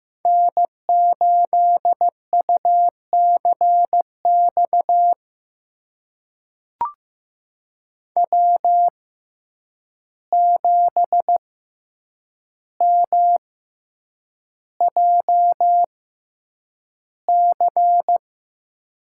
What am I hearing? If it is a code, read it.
Morse code, heard as a keyed tone, about 15 wpm, spaced as for 8 wpm, 700 Hz. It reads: N8UCX W7MJC
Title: USA Call Signs - 5 Characters - 6x Character Spacing - 15wpm